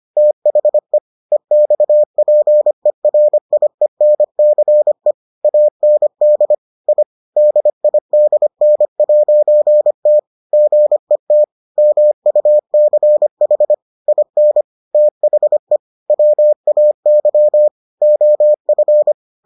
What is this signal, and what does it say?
Morse code, heard as a keyed tone, about 25 wpm, 600 Hz. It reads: THE EXPERIENCE AND I DIDN'T GET MUCH IN THE WAY OF